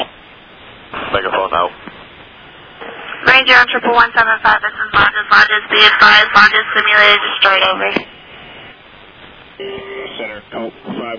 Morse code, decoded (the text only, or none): none